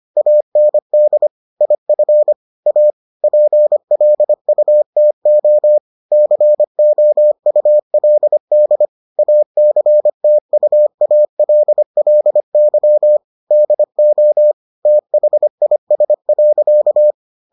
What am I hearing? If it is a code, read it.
Morse code, heard as a keyed tone, about 25 wpm, 600 Hz. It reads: AND IF A PLUTO COULD ACTUALLY DO THIS.